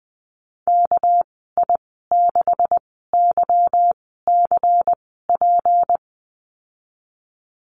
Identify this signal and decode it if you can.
Morse code, heard as a keyed tone, about 20 wpm, spaced as for 15 wpm, 700 Hz. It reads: KI6YCP
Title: 219 Lesson 35g - Number 0 - Call Signs 20wpm